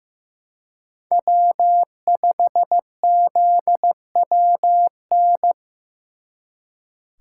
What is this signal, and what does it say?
Morse code, heard as a keyed tone, about 18 wpm, 700 Hz. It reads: W5ZWN
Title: Speed Racing - USA 1x3 Call Signs 15wpm